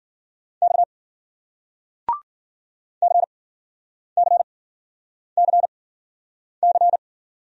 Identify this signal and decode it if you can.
Morse code, heard as a keyed tone, about 54 wpm, spaced as for 9 wpm, 700 Hz. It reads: X CCCC